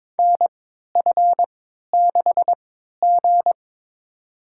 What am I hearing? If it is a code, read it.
Morse code, heard as a keyed tone, about 22 wpm, spaced as for 13 wpm, 700 Hz. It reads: NF6G
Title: USA Call Signs - 3x Character Spacing - 22wpm